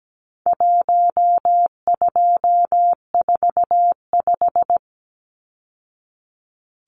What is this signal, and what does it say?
Morse code, heard as a keyed tone, about 17 wpm, 700 Hz. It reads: 1245
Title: Numbers - Contest Style - 4-Digit 17wpm